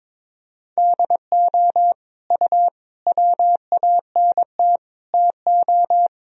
Morse code, heard as a keyed tone, about 22 wpm, 700 Hz. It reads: DO U WANT TO
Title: Sentences from Top 300 Words and Abbreviations 22wpm